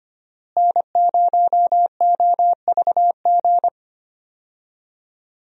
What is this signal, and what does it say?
Morse code, heard as a keyed tone, about 25 wpm, 700 Hz. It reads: N0OVG
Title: USA Call Signs - 5 Characters - Repeated 3x - 25wpm